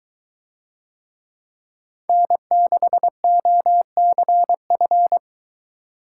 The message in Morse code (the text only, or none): N6OCF